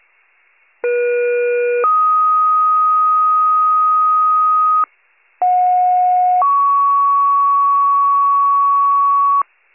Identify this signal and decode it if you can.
Two-tone page, heard as a one-second tone, then a three-second tone, about 480 then 1250 Hz, about 730 then 1120 Hz. It is heard from 0.8 to 9.4 s.